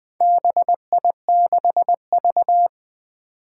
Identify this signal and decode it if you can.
Morse code, heard as a keyed tone, about 20 wpm, 700 Hz. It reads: BI6V
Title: Four Challenging Letter-Number Combinations 20wpm